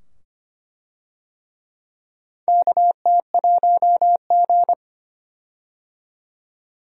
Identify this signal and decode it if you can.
Morse code, heard as a keyed tone, about 25 wpm, 700 Hz. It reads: KT1G
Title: USA Call Signs - 4 Characters - 25wpm